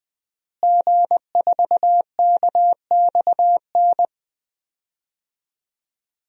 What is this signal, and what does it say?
Morse code, heard as a keyed tone, about 20 wpm, 700 Hz. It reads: G4KXN
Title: DXCC-100 Call Signs - 5 Characters - Repeated 2x - 20wpm